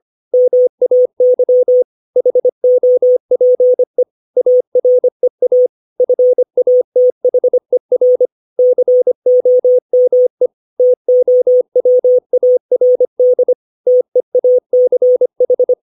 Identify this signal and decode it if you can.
Morse code, heard as a keyed tone, about 25 wpm, 500 Hz. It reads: MAY HOPE AREA FATHER COME TOWARD TEACH
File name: 632 words / 632 Words 25WPM